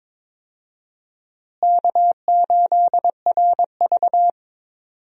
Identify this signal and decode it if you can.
Morse code, heard as a keyed tone, about 22 wpm, 700 Hz. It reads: K8RV
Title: USA Call Signs - 4 Characters - Repeated 2x - 22wpm